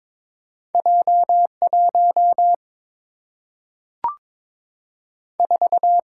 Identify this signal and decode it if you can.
Morse code, heard as a keyed tone, about 22 wpm, 700 Hz. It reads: J 1 4
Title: Single-Letter Single-Number 22wpm